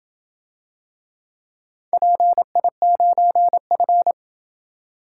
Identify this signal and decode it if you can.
Morse code, heard as a keyed tone, about 27 wpm, 700 Hz. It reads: PI9F